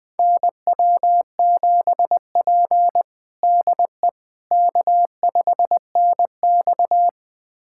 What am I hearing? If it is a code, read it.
Morse code, heard as a keyed tone, about 21 wpm, 700 Hz. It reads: NW7P DE K5NX
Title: USA Callsign DE Callsign - 4 Characters - 20wpm